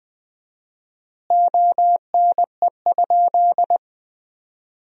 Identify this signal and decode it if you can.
Morse code, heard as a keyed tone, about 21 wpm, 700 Hz. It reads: ONE?
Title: 4x Word-Spacing - Sentences from Top 100 Words - 20wpm